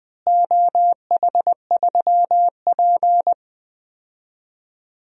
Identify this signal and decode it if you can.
Morse code, heard as a keyed tone, about 20 wpm, 700 Hz. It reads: OH3P
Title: DXCC-100 Call Signs - 4 Characters - Repeated 2x - 20wpm